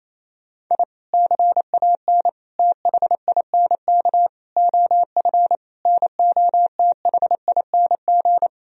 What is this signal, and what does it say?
Morse code, heard as a keyed tone, about 28 wpm, 700 Hz. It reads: I CAN THINK OF NOTHING
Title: Sentences from Top 1000 Words - Review Part 1 - 28wpm